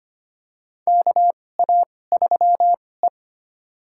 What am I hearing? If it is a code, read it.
Morse code, heard as a keyed tone, about 25 wpm, spaced as for 18 wpm, 700 Hz. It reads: KA3E
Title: USA Call Signs - 4 Characters - 2x Character Spacing - 25wpm